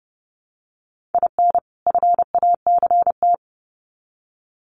Morse code, heard as a keyed tone, about 30 wpm, 700 Hz. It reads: IN FACT
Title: Sets of 2 Words 30wpm